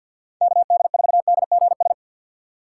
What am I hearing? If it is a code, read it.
Morse code, heard as a keyed tone, about 48 wpm, 690 Hz. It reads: KD4DGS
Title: USA Call Signs - 6 Characters - 3x Character Spacing - 50wpm